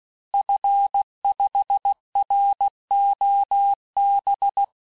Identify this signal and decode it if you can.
Morse code, heard as a keyed tone, about 16 wpm, 800 Hz. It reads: F5ROB
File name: cours-du-lundi files / cours-1592--teste-FIN-de-QSO-vitesse-15mpm